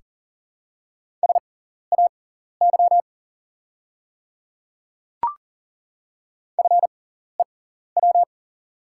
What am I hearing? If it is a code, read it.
Morse code, heard as a keyed tone, about 40 wpm, spaced as for 14 wpm, 690 Hz. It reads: SAY FEW
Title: 6x Character Spacing - Top 300 Words - 40wpm